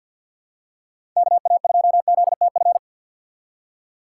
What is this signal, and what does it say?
Morse code, heard as a keyed tone, about 50 wpm, 690 Hz. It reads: KA2ZTF
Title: USA Call Signs - 2x Character Spacing - 50wpm